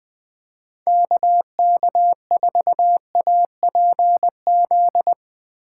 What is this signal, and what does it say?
Morse code, heard as a keyed tone, about 20 wpm, 700 Hz. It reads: KK4APZ